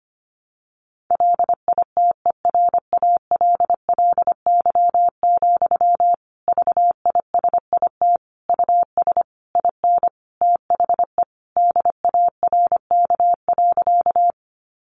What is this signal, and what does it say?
Morse code, heard as a keyed tone, about 25 wpm, 700 Hz. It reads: LITERALLY, VISIT US IN THE DARK.
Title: Sentences from Top 1400 Words 25wpm